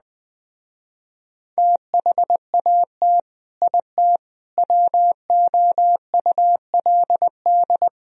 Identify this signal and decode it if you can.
Morse code, heard as a keyed tone, about 20 wpm, 700 Hz. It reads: THAT IT WOULD
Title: Speed-Racing - Sets of 4 Words 17wpm